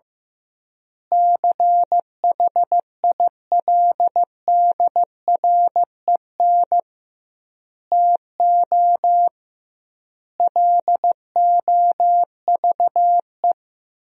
Morse code, heard as a keyed tone, about 15 wpm, 700 Hz. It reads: CHILDREN TO LOVE
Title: 2x Word-Spacing - Sentences from Top 400 Words - 15wpm